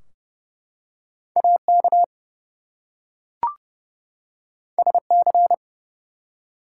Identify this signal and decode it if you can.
Morse code, heard as a keyed tone, about 30 wpm, 700 Hz. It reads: AK SC
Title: US States and Canadian Provinces 30wpm